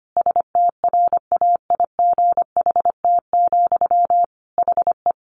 Morse code, heard as a keyed tone, about 25 wpm, 700 Hz. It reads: STRAIGHT, HE
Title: Sentences from Top 1000 Words 25wpm